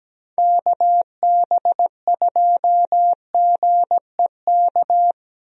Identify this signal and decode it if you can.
Morse code, heard as a keyed tone, about 17 wpm, 700 Hz. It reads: KB2GEK
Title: USA Call Signs - 6 Characters - Repeated 2x - 17wpm